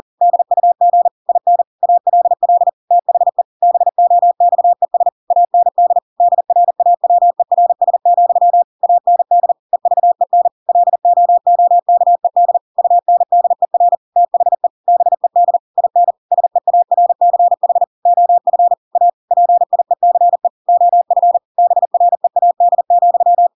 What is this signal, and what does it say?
Morse code, heard as a keyed tone, about 40 wpm, 700 Hz. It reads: DUG IN ALL THE BOXES AND DRAWERS, AND EVEN LOOKED UNDER THE BED IN SEARCH OF A PIECE OF BREAD,
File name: Pinocchio / Pinocchio ch5 40 40 ik1hge